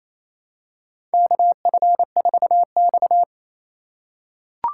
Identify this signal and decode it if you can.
Morse code, heard as a keyed tone, about 27 wpm, 690 Hz. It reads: KF4X
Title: USA Call Signs - 4 Characters - 2x Character Spacing - 28wpm